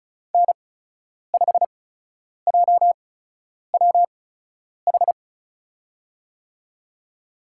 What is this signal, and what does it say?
Morse code, heard as a keyed tone, about 34 wpm, spaced as for 10 wpm, 700 Hz. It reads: N5JWH
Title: USA Call Signs - 5 Characters - 8x Character Spacing - 35wpm